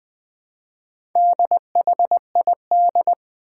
Morse code, heard as a keyed tone, about 20 wpm, 700 Hz. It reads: DHID
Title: Four Challenging Letter-Number Combinations 20wpm